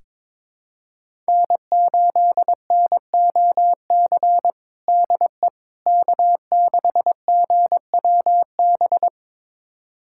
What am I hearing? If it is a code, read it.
Morse code, heard as a keyed tone, about 22 wpm, 700 Hz. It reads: N8NOC DE K6GWB
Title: USA Callsign DE Callsign - 5 Characters - 22wpm